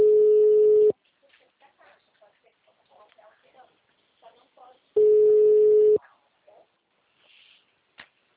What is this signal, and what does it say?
Ringback tone (Europe, 425 Hz), with speech.